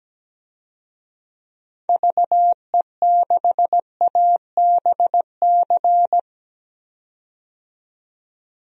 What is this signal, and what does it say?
Morse code, heard as a keyed tone, about 17 wpm, 700 Hz. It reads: VE6ABC
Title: DXCC-100 Call Signs - 6 Characters - 17wpm